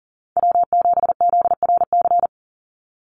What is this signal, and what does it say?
Morse code, heard as a keyed tone, about 40 wpm, 700 Hz. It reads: W7ZRC
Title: Call Signs Repeated 3x - 40wpm